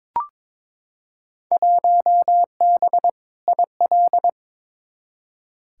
Morse code, heard as a keyed tone, about 22 wpm, 700 Hz. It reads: 1B IL
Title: Field Day Exchanges - 22wpm